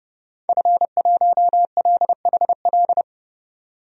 Morse code, heard as a keyed tone, about 30 wpm, 700 Hz. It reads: F1LHL
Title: DXCC-100 Call Signs - 5 Characters - 30wpm